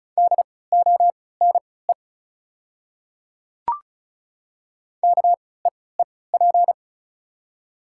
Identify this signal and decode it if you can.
Morse code, heard as a keyed tone, about 35 wpm, spaced as for 20 wpm, 700 Hz. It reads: DONE KEEP
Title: 3x Character Spacing - Top 500 Words - 35wpm